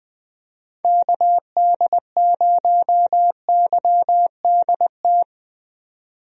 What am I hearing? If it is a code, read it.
Morse code, heard as a keyed tone, about 20 wpm, 700 Hz. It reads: KD0YDT
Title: USA Call Signs - 6 Characters - 20wpm